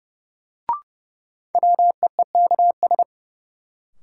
Morse code, heard as a keyed tone, about 32 wpm, 700 Hz. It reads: WEEKS
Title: Sets of 5 Words - New Word Drill - 30wpm